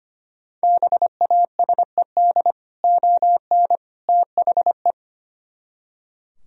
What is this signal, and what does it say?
Morse code, heard as a keyed tone, about 29 wpm, 700 Hz. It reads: BASED ON THE
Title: Speed-Racing - Sets of 3 Words 25wpm